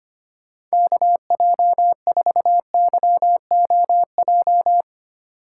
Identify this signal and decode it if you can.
Morse code, heard as a keyed tone, about 25 wpm, 700 Hz. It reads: KJ4YOJ
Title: USA Call Signs - 8x Character Spacing - 25wpm